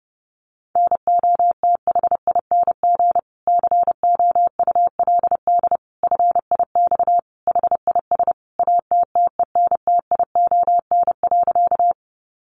Morse code, heard as a keyed tone, about 30 wpm, 700 Hz. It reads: NOTHING COULD FIX HIS ATTENTION.
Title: Sentences from Top 1400 Words 30wpm